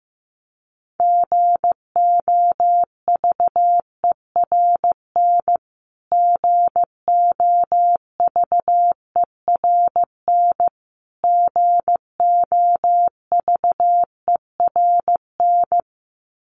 Morse code, heard as a keyed tone, about 15 wpm, 700 Hz. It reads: GOVERN GOVERN GOVERN
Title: Top 500 Words Repeated 15wpm